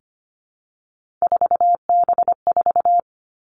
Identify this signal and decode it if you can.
Morse code, heard as a keyed tone, about 25 wpm, 700 Hz. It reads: 4B4
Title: Three Challenging Letter-Number Combinations 25wpm